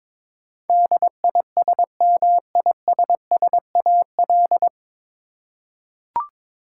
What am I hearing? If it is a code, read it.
Morse code, heard as a keyed tone, about 22 wpm, 700 Hz. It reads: DISMISSAL
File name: 9 Letter Words - First 3 Letters Given - 22wpm